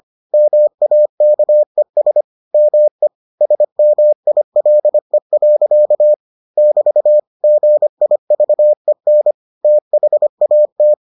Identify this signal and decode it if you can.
Morse code, heard as a keyed tone, about 25 wpm, 600 Hz. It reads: MAKES ME SMILE. <BT> GIVEN THAT